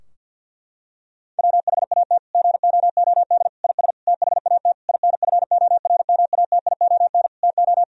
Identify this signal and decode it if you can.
Morse code, heard as a keyed tone, about 50 wpm, 700 Hz. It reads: WHAT GOOD IS THAT INFORMATION TO
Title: Sentences from Top 1000 Words - Review Part 3 - 50wpm